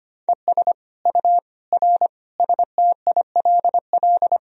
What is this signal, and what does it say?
Morse code, heard as a keyed tone, about 25 wpm, 700 Hz. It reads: ES U R STILL
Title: Sentences from Top 400 Words and Abbreviations 25wpm